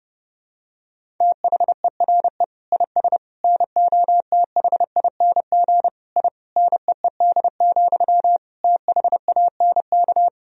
Morse code, heard as a keyed tone, about 32 wpm, 700 Hz. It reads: THERE IS NOTHING I NEED, THANK
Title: Sentences from Top 1000 Words - Review Part 1 - 30wpm